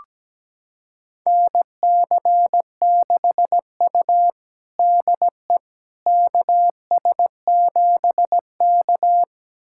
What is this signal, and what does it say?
Morse code, heard as a keyed tone, about 17 wpm, 700 Hz. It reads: NC6U DE KS7K
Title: USA Callsign DE Callsign - 4 Characters - 17wpm